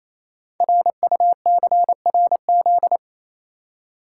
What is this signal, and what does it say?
Morse code, heard as a keyed tone, about 29 wpm, 700 Hz. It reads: RUCRZ
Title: Code Groups - 5 Letters - 28wpm